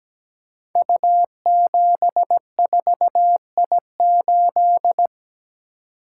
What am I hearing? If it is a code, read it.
Morse code, heard as a keyed tone, about 17 wpm, 700 Hz. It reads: U74I8
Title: Code Groups - 5 Letter:Numbers - 17wpm